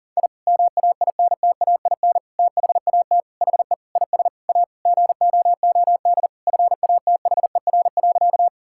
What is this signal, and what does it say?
Morse code, heard as a keyed tone, about 40 wpm, 700 Hz. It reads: I MAINTAIN THAT HE IS A GOOD FATHER.
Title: Sentences from Top 1000 Words - Review Part 5 - 40wpm